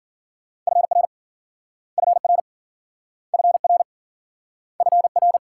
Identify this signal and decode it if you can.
Morse code, heard as a keyed tone, about 53 wpm, 700 Hz. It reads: FR FR FR FR